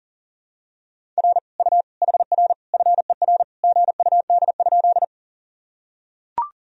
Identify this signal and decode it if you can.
Morse code, heard as a keyed tone, about 40 wpm, 700 Hz. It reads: R U HR FER GUD?
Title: Sentences from Top 200 Words and Abbreviations 40wpm